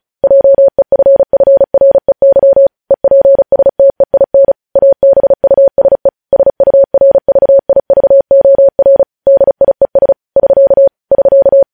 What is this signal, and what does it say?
Morse code, heard as a keyed tone, about 36 wpm, 550 Hz. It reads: JEFFREY EPSTEIN ABUSE SURVIVOR DIES <SK> <SK>
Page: News headlines in Morse code at 35 wpm